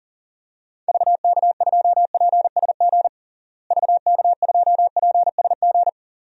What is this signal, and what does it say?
Morse code, heard as a keyed tone, about 37 wpm, 700 Hz. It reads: VK2PSG VK2PSG
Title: DXCC-100 Call Signs - 6 Characters - Repeated 2x - 40wpm